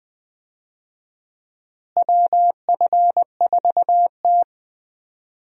Morse code, heard as a keyed tone, about 20 wpm, 700 Hz. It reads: WF4T
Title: USA Call Signs - 4 Characters - Repeated 3x - 20wpm